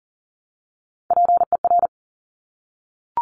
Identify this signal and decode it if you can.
Morse code, heard as a keyed tone, about 41 wpm, 690 Hz. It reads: PER
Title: Top Three Letter Combinations in Words 40wpm